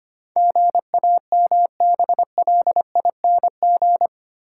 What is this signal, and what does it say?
Morse code, heard as a keyed tone, about 25 wpm, 700 Hz. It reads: GAMBLING
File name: 8 Letter Words - First 3 Letters Given - 25wpm